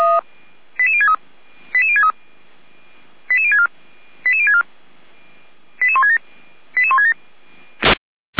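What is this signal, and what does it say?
Five-tone selective call (ZVEI), heard as quick runs of five tones, tones of about 80 ms. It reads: 88063 88063 88064 88064 88167 88167
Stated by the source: ZVEI five-tone call 88352, received on SDR.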